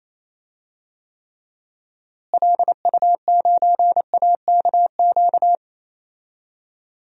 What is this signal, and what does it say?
Morse code, heard as a keyed tone, about 28 wpm, 700 Hz. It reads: LU9AKQ
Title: DXCC-100 Call Signs - 6 Characters - Repeated 3x - 28wpm